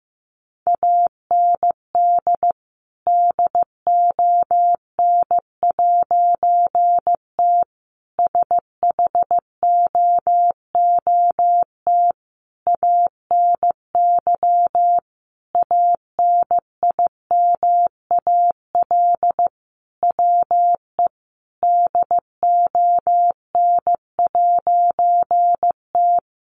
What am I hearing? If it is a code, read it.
Morse code, heard as a keyed tone, about 14 wpm, 700 Hz. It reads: AND DON'T SHOOT ANY ANIMAL WE DON'T
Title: A Sound Of Thunder 15wpm